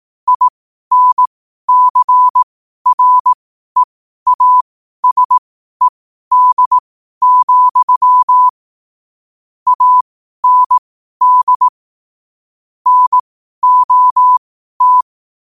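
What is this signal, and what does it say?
Morse code, heard as a keyed tone, about 18 wpm, spaced as for 12 wpm, 1000 Hz. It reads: INCREASED, AND NOT
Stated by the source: Morse practice sent at 12 wpm.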